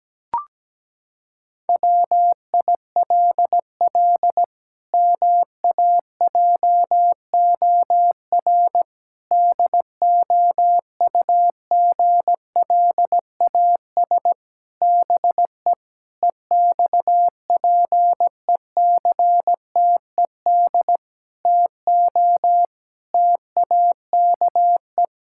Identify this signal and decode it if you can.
Morse code, heard as a keyed tone, about 17 wpm, 700 Hz. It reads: WILL MAJOR DOUGLAS BE EXPECTED TO TAKE
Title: Pangram Challenge v2 - 17wpm